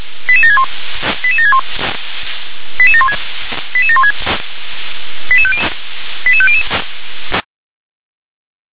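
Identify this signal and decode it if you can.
Five-tone selective call (ZVEI), heard as quick runs of five tones, tones of about 70 ms. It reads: 88761 88761 88716 88716 88500
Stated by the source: ZVEI five-tone selective call 88795 received by SDR.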